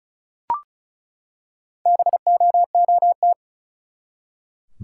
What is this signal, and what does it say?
Morse code, heard as a keyed tone, about 36 wpm, 700 Hz. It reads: BOOT
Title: Four Letter Words - 35wpm